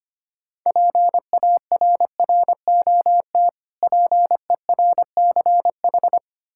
Morse code, heard as a keyed tone, about 25 wpm, 690 Hz. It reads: PARROT PERCH
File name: practice / 25wpm / Difficult Words with Hints - 25wpm